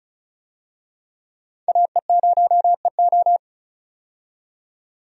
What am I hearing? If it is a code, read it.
Morse code, heard as a keyed tone, about 36 wpm, 700 Hz. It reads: AE0EO
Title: USA Call Signs - 5 Characters - 35wpm